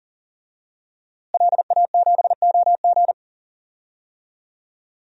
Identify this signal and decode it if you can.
Morse code, heard as a keyed tone, about 40 wpm, 700 Hz. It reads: LA7OG
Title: DXCC-100 Call Signs - 5 Characters - Repeated 3x - 40wpm